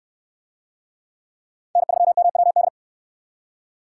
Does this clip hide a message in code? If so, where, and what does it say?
Yes: Morse code from 1.8 to 2.7 s, heard as a keyed tone, about 69 wpm, 700 Hz. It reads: N3MWD